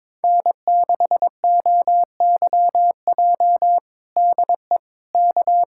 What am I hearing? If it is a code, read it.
Morse code, heard as a keyed tone, about 22 wpm, 700 Hz. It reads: N6OYJ DE K
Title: USA Callsign DE Callsign - 5 Characters - 22wpm